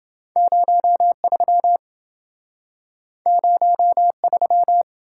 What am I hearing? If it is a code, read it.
Morse code, heard as a keyed tone, about 27 wpm, 700 Hz. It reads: 03 03